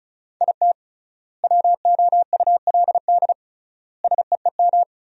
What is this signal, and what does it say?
Morse code, heard as a keyed tone, about 36 wpm, 700 Hz. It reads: IT WOULD SEEM
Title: 3x Word-Spacing - Sentences from Top 300 Words - 35wpm